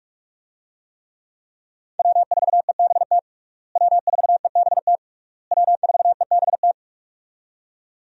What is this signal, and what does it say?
Morse code, heard as a keyed tone, about 42 wpm, 700 Hz. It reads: W4EBT W4EBT W4EBT
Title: USA Call Signs - 5 Characters - Repeated 3x - 45wpm